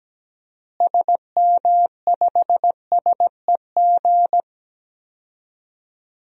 Morse code, heard as a keyed tone, about 17 wpm, 700 Hz. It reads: SM5SEG
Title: DXCC-100 Call Signs - 6 Characters - Repeated 3x - 17wpm